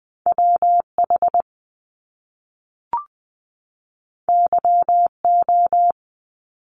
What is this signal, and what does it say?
Morse code, heard as a keyed tone, about 20 wpm, 700 Hz. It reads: WH YO
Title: Top Two Letter Combinations in Words 20wpm